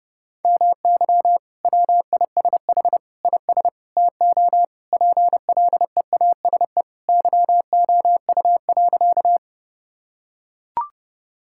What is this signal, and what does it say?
Morse code, heard as a keyed tone, about 30 wpm, 700 Hz. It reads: MY WISH IS TO PLEASE YOU.
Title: Sentences from Top 1600 Words 30wpm